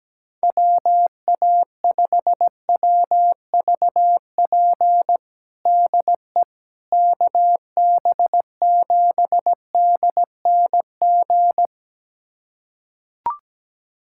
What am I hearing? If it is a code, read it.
Morse code, heard as a keyed tone, about 17 wpm, 700 Hz. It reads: WA5WVP DE KB7DNG
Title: USA Callsign DE Callsign - 6 Characters - 17wpm